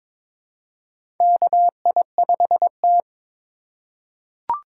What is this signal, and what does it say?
Morse code, heard as a keyed tone, about 22 wpm, 700 Hz. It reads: KI5T
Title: USA Call Signs - 4 Characters - 4x Character Spacing - 22wpm